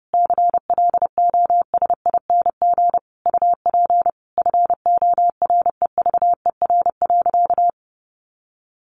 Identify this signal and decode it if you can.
Morse code, heard as a keyed tone, about 30 wpm, 700 Hz. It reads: CLOSING UP FOREVER.
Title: A Sound Of Thunder 30wpm